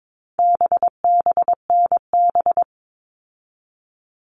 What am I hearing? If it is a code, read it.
Morse code, heard as a keyed tone, about 22 wpm, 700 Hz. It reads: BBNB
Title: All Cut Numbers - 4-Digit 22wpm